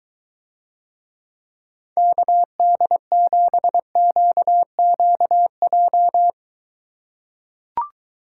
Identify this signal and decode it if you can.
Morse code, heard as a keyed tone, about 23 wpm, 700 Hz. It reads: KD7QQJ